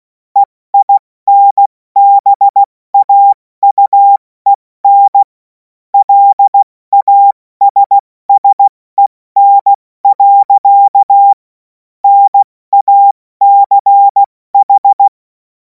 Morse code, heard as a keyed tone, about 16 wpm, 800 Hz. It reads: EINBAUEN LASSEN. NACH
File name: Morsekurs / Witze / Witz 28. April 2026 70 F80